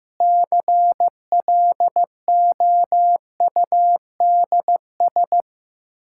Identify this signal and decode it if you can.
Morse code, heard as a keyed tone, about 15 wpm, 700 Hz. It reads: CLOUDS
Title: Practice QSO - Exchange 2 - Info - 15wpm